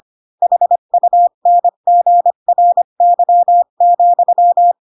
Morse code, heard as a keyed tone, about 25 wpm, 700 Hz. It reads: HUNGRY,
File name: Pinocchio / Pinocchio ch7 20 25 ik1hge